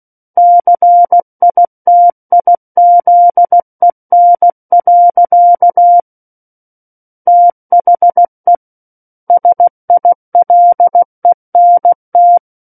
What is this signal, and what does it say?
Morse code, heard as a keyed tone, about 16 wpm, 700 Hz. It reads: CITIZEN. THE SILENT
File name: Book One / 07-Chapter 7 - 16 WPM